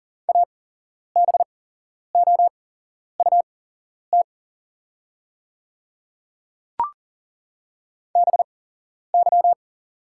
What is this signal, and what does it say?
Morse code, heard as a keyed tone, about 40 wpm, spaced as for 11 wpm, 700 Hz. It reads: ABOUT BY
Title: 8x Character Spacing - Top 100 Words - 40wpm